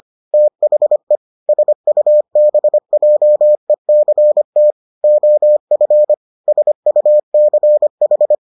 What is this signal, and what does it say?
Morse code, heard as a keyed tone, about 25 wpm, 600 Hz. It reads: THE SUBJECT OF SUCH